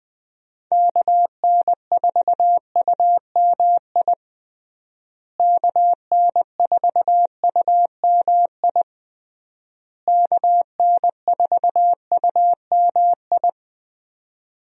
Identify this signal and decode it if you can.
Morse code, heard as a keyed tone, about 20 wpm, 700 Hz. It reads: KN4UMI KN4UMI KN4UMI
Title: USA Call Signs - 6 Characters - Repeated 3x - 20wpm